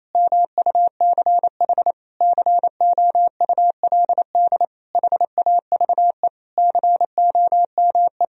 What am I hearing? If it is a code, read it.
Morse code, heard as a keyed tone, about 27 wpm, 700 Hz. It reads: MUCH COULD HAVE COME
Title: Sentences from Top 1000 Words - Review Part 4 - 28wpm